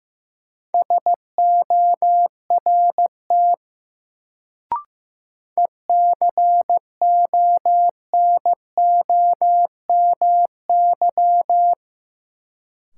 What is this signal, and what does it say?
Morse code, heard as a keyed tone, about 15 wpm, 700 Hz. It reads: SORT ECONOMY
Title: Sentences from Top 1000 Words - Review Part 3 - New Word Drill - 15wpm